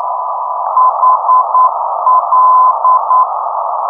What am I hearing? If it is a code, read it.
Morse code, heard as a keyed tone, about 9 wpm, 1030 Hz, with noise. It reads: HL
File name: ndbs